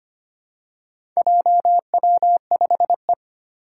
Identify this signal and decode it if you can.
Morse code, heard as a keyed tone, about 25 wpm, 700 Hz. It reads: JW5E
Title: Four Challenging Letter-Number Combinations 25wpm